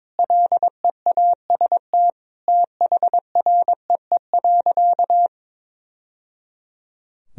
Morse code, heard as a keyed tone, about 22 wpm, 700 Hz. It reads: LEAST THREE.
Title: Sentences from Top 1000 Words - Review Part 1 - 22wpm